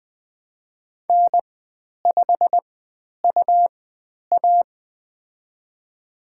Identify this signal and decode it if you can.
Morse code, heard as a keyed tone, about 20 wpm, spaced as for 10 wpm, 700 Hz. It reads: N5UA
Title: 109 Lesson 18g - Letter m - Call Signs 20wpm